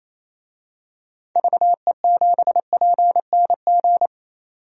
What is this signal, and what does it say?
Morse code, heard as a keyed tone, about 27 wpm, 700 Hz. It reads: VE7PNG
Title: DXCC-100 Call Signs - 6 Characters - 28wpm